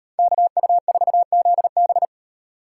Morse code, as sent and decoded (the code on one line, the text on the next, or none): -.- ..- ....- --.. -...
KU4ZB